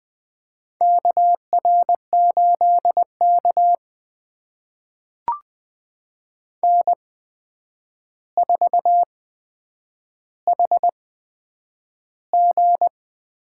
Morse code, heard as a keyed tone, about 20 wpm, spaced as for 9 wpm, 700 Hz. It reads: KR8K N4HG
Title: USA Call Signs - 4 Characters - 8x Character Spacing - 20wpm